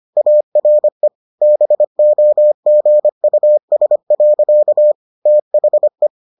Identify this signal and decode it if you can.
Morse code, heard as a keyed tone, about 25 wpm, 600 Hz. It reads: ARE BOGUS. THE